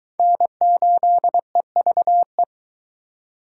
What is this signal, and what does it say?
Morse code, heard as a keyed tone, about 22 wpm, 700 Hz. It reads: N8EVE